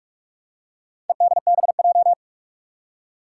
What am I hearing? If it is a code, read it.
Morse code, heard as a keyed tone, about 46 wpm, 700 Hz. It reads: EDBJ
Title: Four Challenging Letter-Number Combinations 45wpm